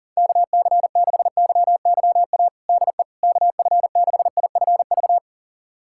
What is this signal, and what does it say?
Morse code, heard as a keyed tone, about 40 wpm, 700 Hz. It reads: KC6YYA DE KF6IFV
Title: USA Callsign DE Callsign - 6 Characters - 40wpm